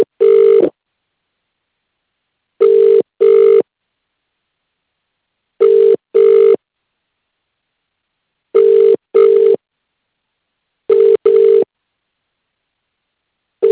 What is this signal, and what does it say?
Ringback tone (UK), 400 + 450 Hz.